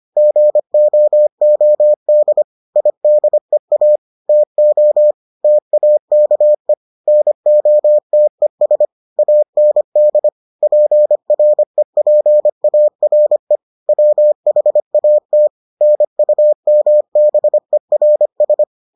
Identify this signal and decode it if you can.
Morse code, heard as a keyed tone, about 25 wpm, 600 Hz. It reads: GOOD IDEA TO TAKE NOTES AND PREPARE WHAT NUMBERS